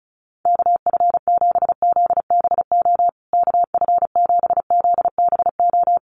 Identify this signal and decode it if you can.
Morse code, heard as a keyed tone, about 34 wpm, 700 Hz. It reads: KF7ZBO KF7ZBO
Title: USA Call Signs Repeated 3x 35wpm